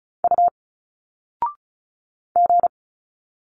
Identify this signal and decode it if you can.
Morse code, heard as a keyed tone, about 35 wpm, 700 Hz. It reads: U G
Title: Single Letters 35wpm